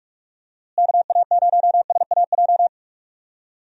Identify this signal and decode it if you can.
Morse code, heard as a keyed tone, about 46 wpm, 700 Hz. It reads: KA0SAJ
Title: USA Call Signs - 6 Characters - Repeated 3x - 45wpm